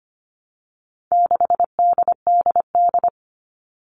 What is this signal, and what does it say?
Morse code, heard as a keyed tone, about 25 wpm, 700 Hz. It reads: 6DDD
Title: All Cut Numbers - 4-Digit 25wpm